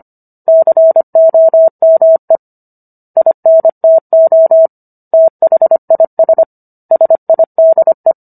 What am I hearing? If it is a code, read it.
Morse code, heard as a keyed tone, about 25 wpm, 650 Hz. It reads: COME INTO THIS SIDE
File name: Book One / 16-Chapter 16 - 25 WPM